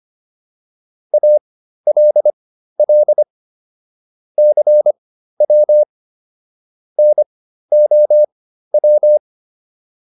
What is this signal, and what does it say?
Morse code, heard as a keyed tone, about 25 wpm, spaced as for 13 wpm, 600 Hz. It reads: ALL CW NOW
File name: PR202 13